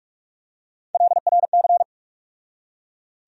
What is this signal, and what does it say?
Morse code, heard as a keyed tone, about 42 wpm, 700 Hz. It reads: LRC
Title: USA Call Sign Postfixes - 3 Letters 45wpm